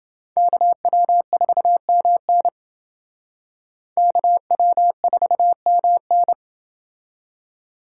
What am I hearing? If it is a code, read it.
Morse code, heard as a keyed tone, about 28 wpm, 700 Hz. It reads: KW4MN KW4MN